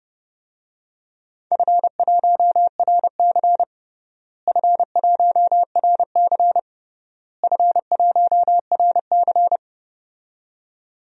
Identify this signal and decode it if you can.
Morse code, heard as a keyed tone, about 30 wpm, 700 Hz. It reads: F1RC F1RC F1RC
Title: DXCC-100 Call Signs - 4 Characters - Repeated 3x - 30wpm